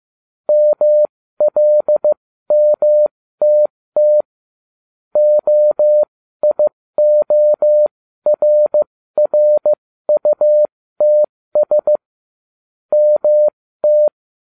Morse code, heard as a keyed tone, about 15 wpm, spaced as for 13 wpm, 600 Hz. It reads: MLMTT OIORRUTS MT